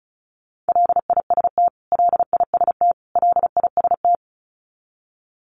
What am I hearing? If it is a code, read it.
Morse code, heard as a keyed tone, about 34 wpm, 700 Hz. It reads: LIST LIST LIST
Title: Top 500 Words Repeated 35wpm